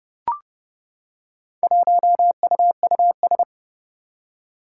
Morse code, heard as a keyed tone, about 29 wpm, 700 Hz. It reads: 1UUS
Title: Four Challenging Letter-Number Combinations 30wpm